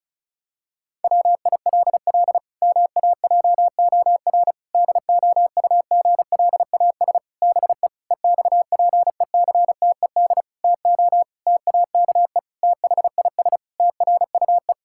Morse code, heard as a keyed tone, about 36 wpm, 700 Hz. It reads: WILL MAJOR DOUGLAS BE EXPECTED TO TAKE THIS TRUE
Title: Pangram Challenge v2 - 35wpm